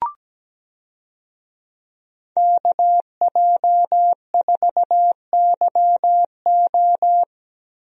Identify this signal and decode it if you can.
Morse code, heard as a keyed tone, about 17 wpm, 700 Hz. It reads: KJ4YO
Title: USA Call Signs - 5 Characters - Repeated 2x - 17wpm